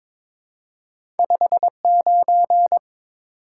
Morse code, heard as a keyed tone, about 22 wpm, 700 Hz. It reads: 59
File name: practice / 22wpm / Two Letter-Number Combinations - 22wpm